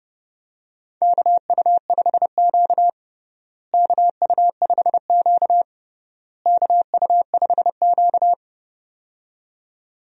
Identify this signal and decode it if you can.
Morse code, heard as a keyed tone, about 32 wpm, 700 Hz. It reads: KU5Q KU5Q KU5Q
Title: USA Call Signs - 4 Characters - Repeated 3x - 30wpm